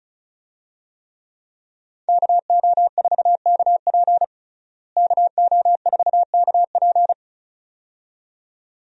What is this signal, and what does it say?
Morse code, heard as a keyed tone, about 34 wpm, 700 Hz. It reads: KO4KP KO4KP
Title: USA Call Signs - 5 Characters - Repeated 2x - 35wpm